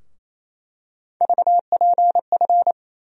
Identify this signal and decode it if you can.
Morse code, heard as a keyed tone, about 27 wpm, 700 Hz. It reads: VPF
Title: USA Call Sign Postfixes - 3 Letters 28wpm